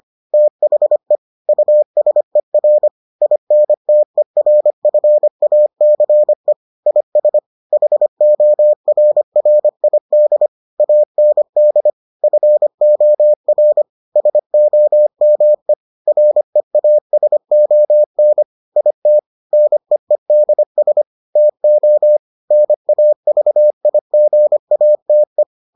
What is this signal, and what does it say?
Morse code, heard as a keyed tone, about 25 wpm, 600 Hz. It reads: THE USER INTERFACE IS HORRID AND FOR SOME REASON IT NEEDS TO NAVIGATE